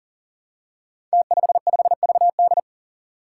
Morse code, heard as a keyed tone, about 39 wpm, 700 Hz. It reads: T55VD
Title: Five Challenging Letter-Number Combinations 40wpm